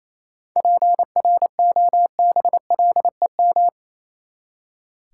Morse code, heard as a keyed tone, about 28 wpm, 700 Hz. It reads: PROBLEM
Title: Seven Letter Words - 28wpm